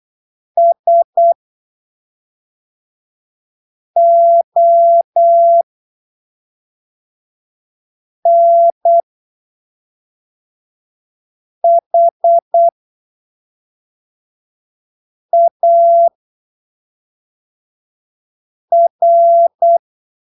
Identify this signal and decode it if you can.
Morse code, heard as a keyed tone, about 8 wpm, spaced as for 3 wpm, 680 Hz. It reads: SONHAR
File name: arquivos / texto / texto 21 08ppm